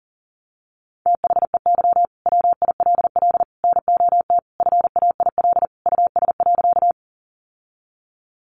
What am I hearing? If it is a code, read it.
Morse code, heard as a keyed tone, about 40 wpm, 700 Hz. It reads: THEY WILL NOT FAIL US.
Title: Sentences from Top 1100 Words 40wpm